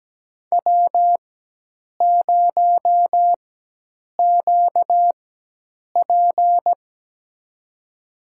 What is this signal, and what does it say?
Morse code, heard as a keyed tone, about 17 wpm, spaced as for 8 wpm, 700 Hz. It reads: W0QP
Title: USA Call Signs - 4 Characters - 4x Character Spacing - 17wpm